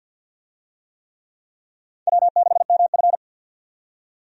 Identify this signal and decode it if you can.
Morse code, heard as a keyed tone, about 52 wpm, 700 Hz. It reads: W6MF
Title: USA Call Signs - 4 Characters - 50wpm